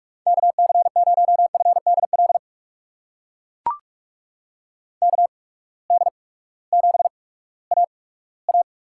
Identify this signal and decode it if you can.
Morse code, heard as a keyed tone, about 44 wpm, spaced as for 20 wpm, 700 Hz. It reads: KC0FDL KD7AA
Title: USA Call Signs - 6 Characters - 8x Character Spacing - 45wpm